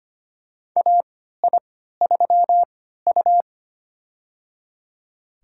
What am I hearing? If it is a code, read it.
Morse code, heard as a keyed tone, about 25 wpm, spaced as for 14 wpm, 700 Hz. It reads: AI3U
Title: USA Call Signs - 4 Characters - 3x Character Spacing - 25wpm